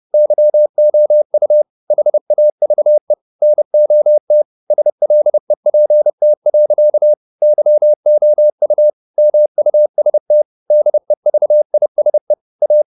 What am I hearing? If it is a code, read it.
Morse code, heard as a keyed tone, about 30 wpm, 600 Hz. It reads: YOU HAVE NOT SLEPT. YOU MUST DEVISE A